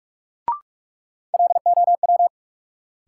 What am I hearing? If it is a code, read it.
Morse code, heard as a keyed tone, about 46 wpm, 700 Hz. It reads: LOW
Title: Sentences from Top 500 Words - New Word Drill - 45wpm